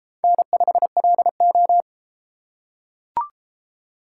Morse code, heard as a keyed tone, about 30 wpm, 700 Hz. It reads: N5LO